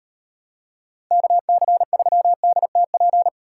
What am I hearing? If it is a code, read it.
Morse code, heard as a keyed tone, about 35 wpm, 700 Hz. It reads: KC3DTP